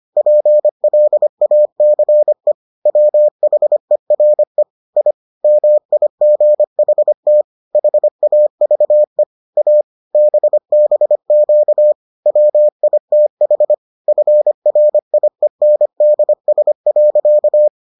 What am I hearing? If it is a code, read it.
Morse code, heard as a keyed tone, about 25 wpm, 600 Hz. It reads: PLACE WHERE I MIGHT HAVE A BBQ WITH FRIENDS.